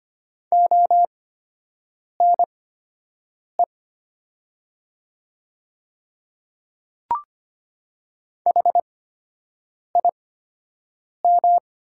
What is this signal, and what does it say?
Morse code, heard as a keyed tone, about 25 wpm, spaced as for 7 wpm, 700 Hz. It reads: ONE HIM
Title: 8x Character Spacing - Top 100 Words - 25wpm